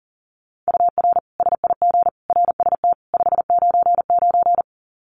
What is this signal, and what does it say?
Morse code, heard as a keyed tone, about 40 wpm, 700 Hz. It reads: UR SIG RST 599
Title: Common QSO Phrases 40wpm